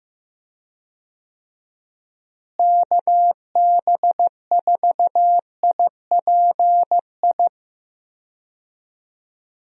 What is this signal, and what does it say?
Morse code, heard as a keyed tone, about 15 wpm, 700 Hz. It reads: KB4IPI
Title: USA Call Signs - 6 Characters - Repeated 2x - 15wpm